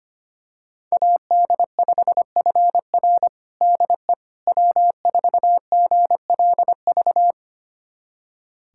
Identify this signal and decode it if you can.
Morse code, heard as a keyed tone, about 25 wpm, 700 Hz. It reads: AD5FR DE W4GLV
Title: USA Callsign DE Callsign - 5 Characters - 25wpm